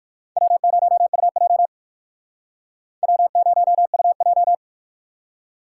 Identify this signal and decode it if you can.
Morse code, heard as a keyed tone, about 57 wpm, 700 Hz. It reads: W0UJ W0UJ